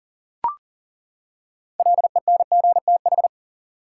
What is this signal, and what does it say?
Morse code, heard as a keyed tone, about 40 wpm, 700 Hz. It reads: LENGTH
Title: Six Letter Words - 40wpm